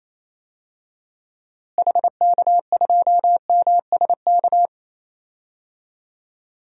Morse code, heard as a keyed tone, about 28 wpm, 700 Hz. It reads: HK2MSK
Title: DXCC-100 Call Signs - 6 Characters - Repeated 3x - 28wpm